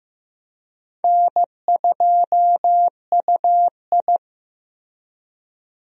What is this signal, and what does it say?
Morse code, heard as a keyed tone, about 16 wpm, 700 Hz. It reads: N2UI
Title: USA Call Signs - 4 Characters - Repeated 3x - 15wpm